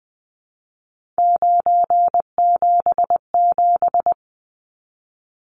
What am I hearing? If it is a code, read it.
Morse code, heard as a keyed tone, about 20 wpm, 700 Hz. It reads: 977
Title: Numbers - 3-Digit 20wpm